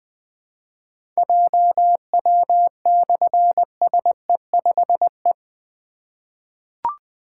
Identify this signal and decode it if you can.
Morse code, heard as a keyed tone, about 20 wpm, 700 Hz. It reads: JW/SE5E
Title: Difficult Call Signs 20wpm